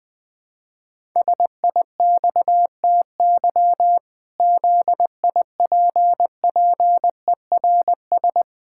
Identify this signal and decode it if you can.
Morse code, heard as a keyed tone, about 21 wpm, 700 Hz. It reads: SIXTY ZIPPERS
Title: Pangram Challenge v2 - 20wpm